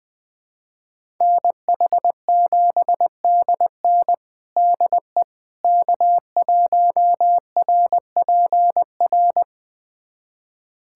Morse code, heard as a keyed tone, about 21 wpm, 700 Hz. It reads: NH7DN DE K1RPR
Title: USA Callsign DE Callsign 20wpm